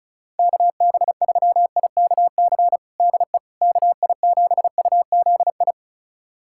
Morse code, heard as a keyed tone, about 36 wpm, 700 Hz. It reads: KB3IKC DE KI7UZI
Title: USA Callsign DE Callsign - 6 Characters - 35wpm